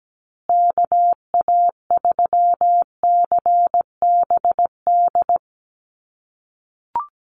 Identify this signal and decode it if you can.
Morse code, heard as a keyed tone, about 17 wpm, 700 Hz. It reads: KA3CBD
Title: USA Call Signs Repeated 3x 17wpm